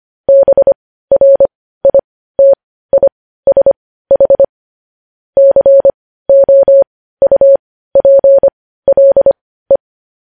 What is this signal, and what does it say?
Morse code, heard as a keyed tone, about 25 wpm, spaced as for 15 wpm, 550 Hz. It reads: BRITISH COUPLE
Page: news headlines in morse code at 15 wpm